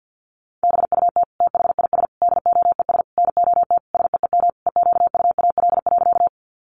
Morse code, heard as a keyed tone, about 50 wpm, 700 Hz. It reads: BUT THIS DOES NOT SEEM EQUAL.
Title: Sentences from Top 1400 Words 50wpm